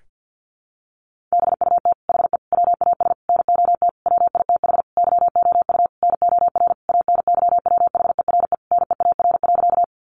Morse code, heard as a keyed tone, about 50 wpm, 700 Hz. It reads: BUT HE WAS NOT WITH YOU NOR ANYWHERE NEAR.
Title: Sentences from Top 1500 Words 50wpm